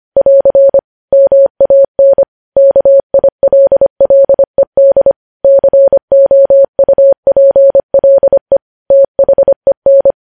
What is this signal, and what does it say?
Morse code, heard as a keyed tone, about 25 wpm, 550 Hz. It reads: <AR> MAN KILLED COUPLE THEN
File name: podcast / News headlines-1x-25x25